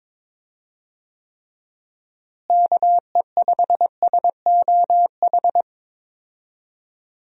Morse code, heard as a keyed tone, about 22 wpm, 700 Hz. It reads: KE5SOH
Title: USA Call Signs - 6 Characters - Repeated 3x - 22wpm